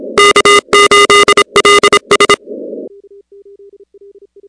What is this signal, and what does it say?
Morse code, heard as a keyed tone, about 26 wpm, 1220 Hz, with noise. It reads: K8LS
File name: contests / 2019-cw-wpx-kt5j-k5pi- -2330 0-15